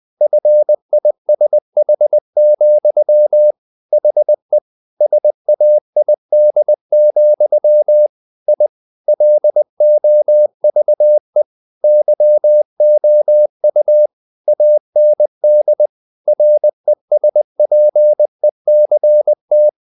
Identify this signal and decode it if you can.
Morse code, heard as a keyed tone, about 20 wpm, 600 Hz. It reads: FISH, HE SAID, I LOVE YOU AND RESPECT
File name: the old man and the sea / the old man and the sea part 1 20wpm